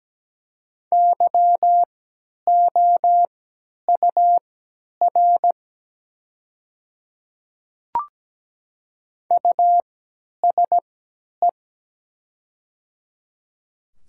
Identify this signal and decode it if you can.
Morse code, heard as a keyed tone, about 17 wpm, spaced as for 10 wpm, 700 Hz. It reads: YOUR USE
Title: 3x Character Spacing - Top 500 Words - 17wpm